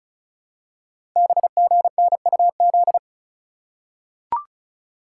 Morse code, heard as a keyed tone, about 36 wpm, 700 Hz. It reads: BGNUZ
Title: Code Groups - 5 Letters - 35wpm